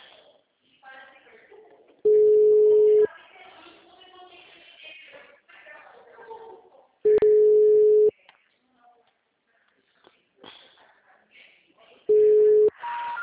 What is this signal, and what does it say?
Ringback tone (Europe, 425 Hz), with speech.